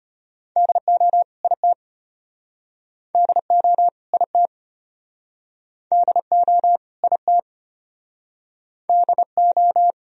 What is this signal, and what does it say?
Morse code, heard as a keyed tone, about 33 wpm, 700 Hz. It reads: DO IT DO IT DO IT DO